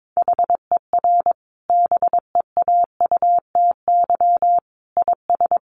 Morse code, heard as a keyed tone, about 22 wpm, 700 Hz. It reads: HER BEAUTY IS
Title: Sentences from Top 1100 Words 22wpm